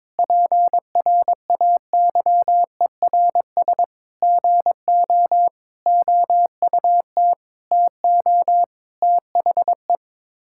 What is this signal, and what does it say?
Morse code, heard as a keyed tone, about 22 wpm, 700 Hz. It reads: PRAYERS GO OUT TO THE
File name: practice / 22wpm / Sets of 8 Words - 22wpm